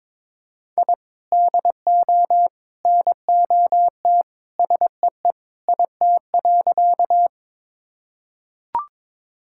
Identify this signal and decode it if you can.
Morse code, heard as a keyed tone, about 22 wpm, 700 Hz. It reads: I DO NOT SEE IT.
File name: practice / 22wpm / Sentences with 3-Word Phrases - 22wpm